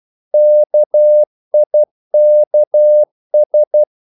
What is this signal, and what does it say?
Morse code, heard as a keyed tone, about 12 wpm, 600 Hz. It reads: KIKS
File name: Korte ord DK 12wpm